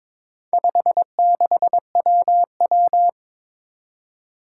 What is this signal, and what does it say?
Morse code, heard as a keyed tone, about 22 wpm, 700 Hz. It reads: 56WW
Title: Four Challenging Letter-Number Combinations 22wpm